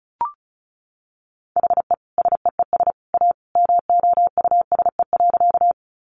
Morse code, heard as a keyed tone, about 34 wpm, 700 Hz. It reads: HE SEES A MOUSE.
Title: Easy Phrases 35wpm